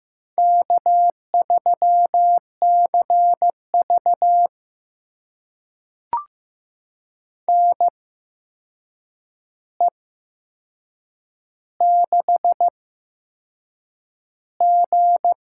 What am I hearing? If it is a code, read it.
Morse code, heard as a keyed tone, about 15 wpm, spaced as for 6 wpm, 700 Hz. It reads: K3CV NE6G
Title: USA Call Signs - 4 Characters - 8x Character Spacing - 15wpm